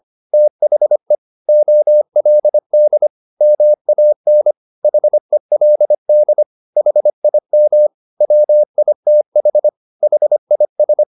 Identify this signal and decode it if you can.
Morse code, heard as a keyed tone, about 25 wpm, 600 Hz. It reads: THE OLD MAN HELD HIM WITH HIS